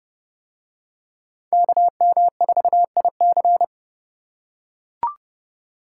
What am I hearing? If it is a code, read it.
Morse code, heard as a keyed tone, about 30 wpm, 700 Hz. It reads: KM4IC